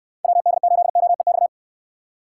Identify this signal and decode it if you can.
Morse code, heard as a keyed tone, about 67 wpm, 700 Hz. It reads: WD8OEX